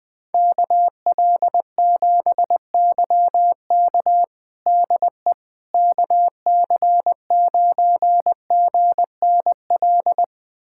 Morse code, heard as a keyed tone, about 21 wpm, 700 Hz. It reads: KL7YK DE KC9GNL
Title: USA Callsign DE Callsign 20wpm